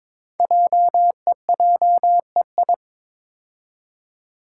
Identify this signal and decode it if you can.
Morse code, heard as a keyed tone, about 23 wpm, 690 Hz. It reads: JEJEI
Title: Code Groups - 5 Letter:Numbers - 22wpm